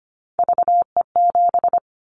Morse code, heard as a keyed tone, about 25 wpm, 700 Hz. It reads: VE7
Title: Callsign Prefixes for Top 100 DXCC - 25wpm